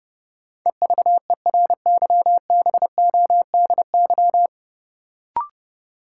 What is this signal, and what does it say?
Morse code, heard as a keyed tone, about 30 wpm, 700 Hz. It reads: EVERYBODY
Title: Sentences from Top 1400 Words - New Word Drill Alt - 30wpm